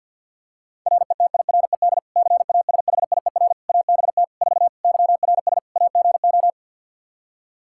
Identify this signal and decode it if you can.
Morse code, heard as a keyed tone, about 52 wpm, 700 Hz. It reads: RETIRED CASHIER ABT 4 YRS AGO
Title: Practice QSO - Exchange 3 - More Info - 50wpm